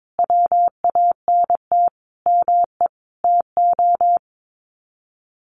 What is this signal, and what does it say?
Morse code, heard as a keyed tone, about 22 wpm, 700 Hz. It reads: WANT ME TO
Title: Sets of 4 Words 22wpm